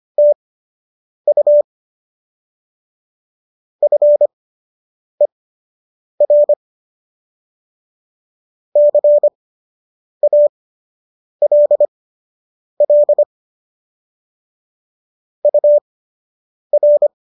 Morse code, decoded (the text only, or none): TU FER CALL UR